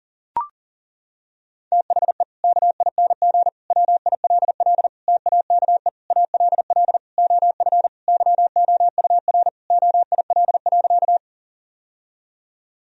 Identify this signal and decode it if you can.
Morse code, heard as a keyed tone, about 40 wpm, 700 Hz. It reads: THE KING WILL TAKE ALL OF YOUR OIL.
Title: Sentences from Top 1000 Words - Review Part 4 - 40wpm